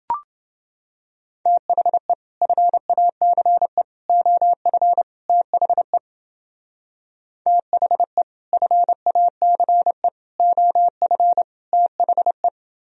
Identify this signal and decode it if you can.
Morse code, heard as a keyed tone, about 30 wpm, 700 Hz. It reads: THE FACE OF THE THE FACE OF THE